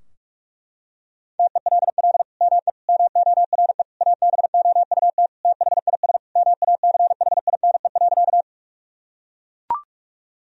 Morse code, heard as a keyed tone, about 42 wpm, 700 Hz. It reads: TELL ME MORE ABOUT THIS MACHINE.
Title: Sentences from Top 1000 Words - Review Part 2 - 45wpm